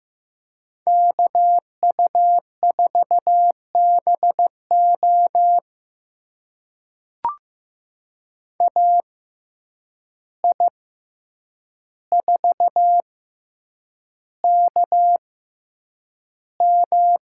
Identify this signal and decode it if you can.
Morse code, heard as a keyed tone, about 15 wpm, spaced as for 8 wpm, 700 Hz. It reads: KU4BO AI4KM
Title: USA Call Signs - 5 Characters - 6x Character Spacing - 15wpm